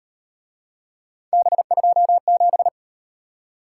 Morse code, heard as a keyed tone, about 37 wpm, 700 Hz. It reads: B27